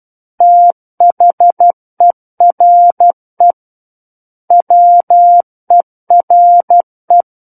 Morse code, heard as a keyed tone, about 12 wpm, 700 Hz. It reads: THERE WERE
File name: Book One / 03-Chapter 3 - 12 WPM